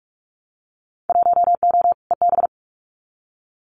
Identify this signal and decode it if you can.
Morse code, heard as a keyed tone, about 44 wpm, 700 Hz. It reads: 1O EB
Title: Winter Field Day Exchanges 45wpm